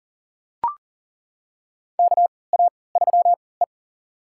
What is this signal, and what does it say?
Morse code, heard as a keyed tone, about 40 wpm, spaced as for 23 wpm, 700 Hz. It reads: KA3E
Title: USA Call Signs - 4 Characters - 3x Character Spacing - 40wpm